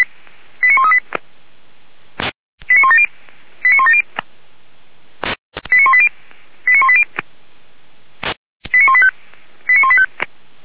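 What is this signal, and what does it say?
Five-tone selective call (ZVEI), heard as quick runs of five tones, tones of about 70 ms. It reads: 89138 89170 89170 89180 89180 89185 89185
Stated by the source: ZVEI five-tone selective call received with an SDR, logged by Feuerwehr Orlamünde.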